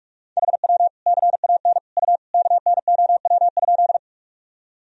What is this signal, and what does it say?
Morse code, heard as a keyed tone, about 44 wpm, 700 Hz. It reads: HW CAN U KNOW?
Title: Sentences from Top 100 Words and Abbreviations 45wpm